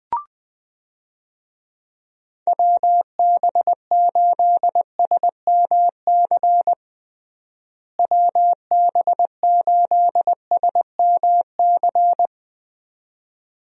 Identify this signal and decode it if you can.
Morse code, heard as a keyed tone, about 20 wpm, 700 Hz. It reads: WB8SMC WB8SMC
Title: USA Call Signs - 6 Characters - Repeated 2x - 20wpm